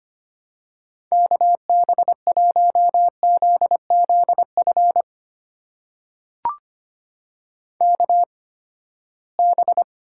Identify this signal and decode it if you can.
Morse code, heard as a keyed tone, about 25 wpm, spaced as for 17 wpm, 700 Hz. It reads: KB1ZZF KB
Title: USA Call Signs - 6 Characters - 8x Character Spacing - 25wpm